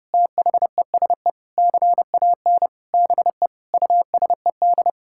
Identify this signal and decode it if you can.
Morse code, heard as a keyed tone, about 30 wpm, 700 Hz. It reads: THESE CAN BE USED
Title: Sentences from Top 1000 Words - Review Part 6 - 30wpm